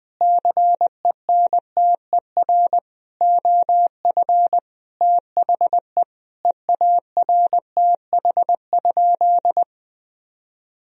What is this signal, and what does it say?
Morse code, heard as a keyed tone, about 20 wpm, 700 Hz. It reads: CENTER OF THE EARTH?
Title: Sentences from Top 1000 Words - Review Part 2 - 20wpm